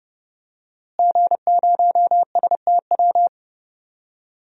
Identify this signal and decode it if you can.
Morse code, heard as a keyed tone, about 30 wpm, 700 Hz. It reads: G0STW
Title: DXCC-100 Call Signs - 5 Characters - Repeated 3x - 30wpm